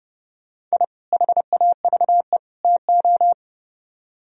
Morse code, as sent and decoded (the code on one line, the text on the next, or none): .. / .... .- ...- . / - ---
I HAVE TO